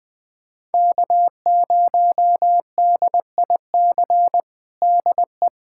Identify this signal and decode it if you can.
Morse code, heard as a keyed tone, about 20 wpm, 700 Hz. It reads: K0DIC DE
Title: USA Callsign DE Callsign - 5 Characters - 20wpm